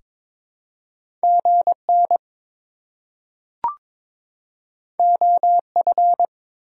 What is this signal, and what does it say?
Morse code, heard as a keyed tone, about 22 wpm, 700 Hz. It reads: GN OF
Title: Top 180 QSO Elements 22wpm